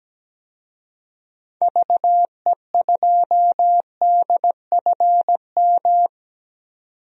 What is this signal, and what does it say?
Morse code, heard as a keyed tone, about 17 wpm, 700 Hz. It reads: VE2DFM
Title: DXCC-100 Call Signs - 6 Characters - 17wpm